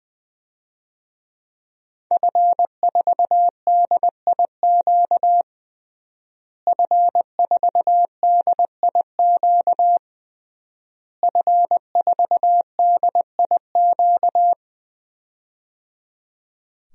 Morse code, heard as a keyed tone, about 20 wpm, 700 Hz. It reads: F4DIQ F4DIQ F4DIQ
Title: DXCC-100 Call Signs - 5 Characters - Repeated 3x - 20wpm